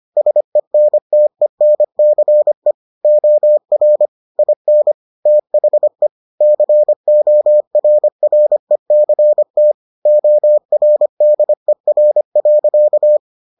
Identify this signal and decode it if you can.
Morse code, heard as a keyed tone, about 25 wpm, 600 Hz. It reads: SENTENCE OR IN THE CORRECT ORDER.